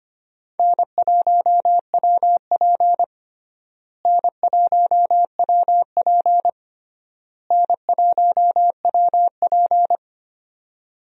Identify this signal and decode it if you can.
Morse code, heard as a keyed tone, about 25 wpm, 700 Hz. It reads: N1WP N1WP N1WP
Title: USA Call Signs - 4 Characters - Repeated 3x - 25wpm